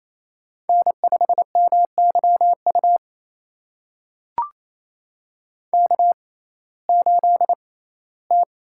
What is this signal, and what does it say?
Morse code, heard as a keyed tone, about 28 wpm, spaced as for 17 wpm, 700 Hz. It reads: N5MYU K8T
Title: USA Call Signs - 5 Characters - 6x Character Spacing - 28wpm